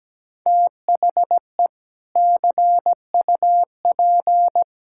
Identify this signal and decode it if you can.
Morse code, heard as a keyed tone, about 17 wpm, 700 Hz. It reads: THE CUP
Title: Sentences from Top 1600 Words 17wpm